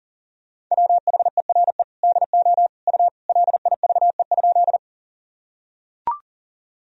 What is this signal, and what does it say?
Morse code, heard as a keyed tone, about 37 wpm, 700 Hz. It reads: WHERE DO U LIVE?
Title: Sentences from Top 200 Words and Abbreviations 40wpm